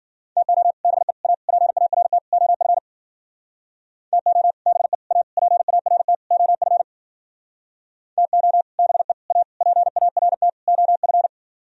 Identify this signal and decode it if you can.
Morse code, heard as a keyed tone, about 50 wpm, 700 Hz. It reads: TO BE A PART OF TO BE A PART OF TO BE A PART OF